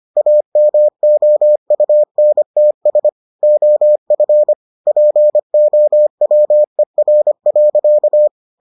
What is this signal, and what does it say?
Morse code, heard as a keyed tone, about 25 wpm, 600 Hz. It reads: AMOUNTS OF POWER.